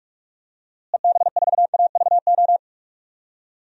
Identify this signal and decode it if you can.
Morse code, heard as a keyed tone, about 46 wpm, 700 Hz. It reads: EB4AVO
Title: DXCC-100 Call Signs - 6 Characters - Repeated 3x - 45wpm